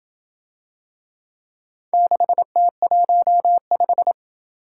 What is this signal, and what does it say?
Morse code, heard as a keyed tone, about 27 wpm, 700 Hz. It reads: 6T15